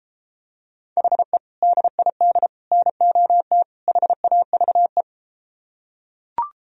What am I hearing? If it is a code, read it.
Morse code, heard as a keyed tone, about 33 wpm, 700 Hz. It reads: HE DID NOT HAVE